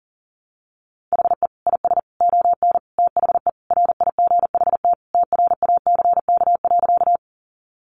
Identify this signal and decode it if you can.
Morse code, heard as a keyed tone, about 40 wpm, 700 Hz. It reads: HE IS ON THE RIGHT TRACK.
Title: Sentences from Top 1100 Words 40wpm